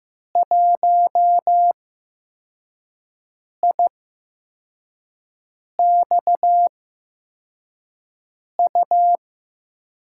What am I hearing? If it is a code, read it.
Morse code, heard as a keyed tone, about 15 wpm, spaced as for 4 wpm, 700 Hz. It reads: 1IXU
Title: USA Call Signs - 5 Characters - 8x Character Spacing - 15wpm